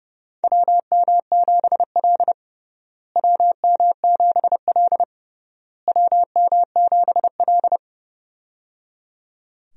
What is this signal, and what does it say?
Morse code, heard as a keyed tone, about 30 wpm, 700 Hz. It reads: WM7L WM7L WM7L
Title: USA Call Signs - 4 Characters - Repeated 3x - 30wpm